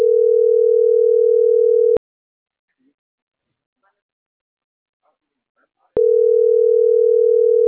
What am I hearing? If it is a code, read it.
Ringback tone (North America), 440 + 480 Hz.